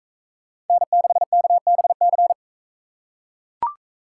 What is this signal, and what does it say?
Morse code, heard as a keyed tone, about 40 wpm, 700 Hz. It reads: N6KBC